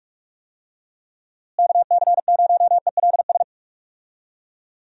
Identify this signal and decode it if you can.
Morse code, heard as a keyed tone, about 41 wpm, 700 Hz. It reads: KC0ELS
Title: USA Call Signs - 6 Characters - Repeated 3x - 45wpm